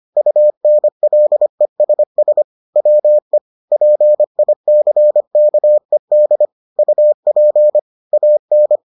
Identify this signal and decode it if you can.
Morse code, heard as a keyed tone, about 25 wpm, 600 Hz. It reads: UNLESS WE PICKED UP AN